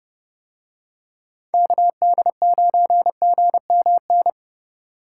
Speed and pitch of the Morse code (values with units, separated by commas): 31 wpm, 700 Hz